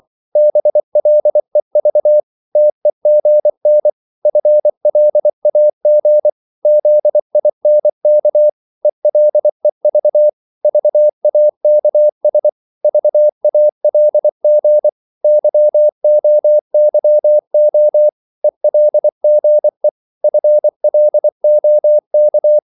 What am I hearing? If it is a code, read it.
Morse code, heard as a keyed tone, about 24 wpm, 600 Hz. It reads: BLEV TEGN FLAG ZINK ELEV VAKS VALG YOYO ELGE FLOK
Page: Korte danske ord 24wpm